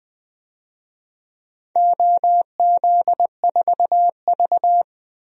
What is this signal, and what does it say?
Morse code, heard as a keyed tone, about 20 wpm, 700 Hz. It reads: OZ4V
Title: DXCC-100 Call Signs - 4 Characters - 20wpm